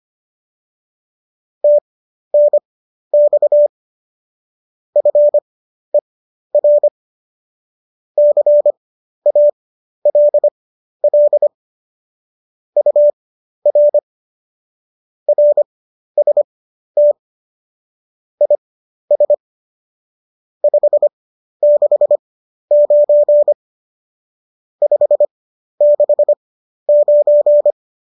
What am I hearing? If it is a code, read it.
Morse code, heard as a keyed tone, about 25 wpm, spaced as for 12 wpm, 590 Hz. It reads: TNX FER CALL UR RST IS 569 569
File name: qso002 12